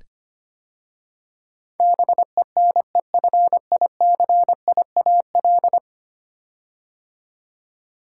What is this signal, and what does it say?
Morse code, heard as a keyed tone, about 25 wpm, 700 Hz. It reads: BENEFICIAL
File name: 10 Letter Words - First 3 Letters Given - 25wpm